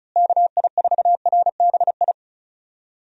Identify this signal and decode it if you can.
Morse code, heard as a keyed tone, about 36 wpm, 700 Hz. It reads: KI4RBI
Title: USA Call Signs - 6 Characters - Repeated 2x - 35wpm